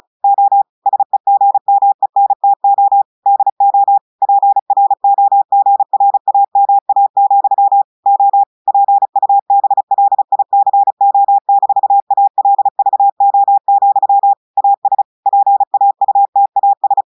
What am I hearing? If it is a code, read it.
Morse code, heard as a keyed tone, about 35 wpm, 800 Hz. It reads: O SEGMENTO DO PROGRAMA, O PUBLICO-ALVO, AS PAUTAS
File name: audios / portugais / portugais texte en clair 2 vitesse 35 mots minutes